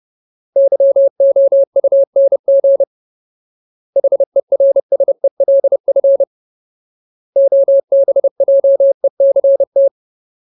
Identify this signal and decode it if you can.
Morse code, heard as a keyed tone, about 30 wpm, 550 Hz. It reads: YOUNG HERSELF OBJECT
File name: Ultimate Word List / 30WPM Ultimate